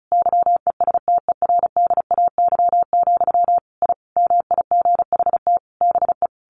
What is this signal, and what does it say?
Morse code, heard as a keyed tone, about 34 wpm, 700 Hz. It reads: YESTERDAY, I MIGHT BE
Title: A Sound Of Thunder 35wpm